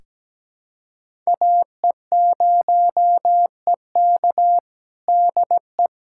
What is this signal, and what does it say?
Morse code, heard as a keyed tone, about 17 wpm, 700 Hz. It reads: AE0EK DE
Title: USA Callsign DE Callsign - 5 Characters - 17wpm